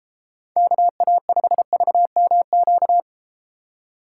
Morse code, heard as a keyed tone, about 33 wpm, 700 Hz. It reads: KA5VMQ